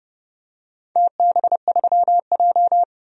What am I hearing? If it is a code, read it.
Morse code, heard as a keyed tone, about 30 wpm, 700 Hz. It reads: TB3J
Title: DXCC-100 Call Signs - 4 Characters - 30wpm